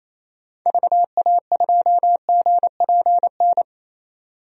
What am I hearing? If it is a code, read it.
Morse code, heard as a keyed tone, about 28 wpm, 700 Hz. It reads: VA2GPN
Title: DXCC-100 Call Signs - 6 Characters - 28wpm